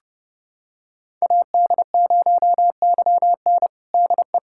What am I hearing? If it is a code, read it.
Morse code, heard as a keyed tone, about 32 wpm, 700 Hz. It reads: AD0YN DE
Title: USA Callsign DE Callsign 30wpm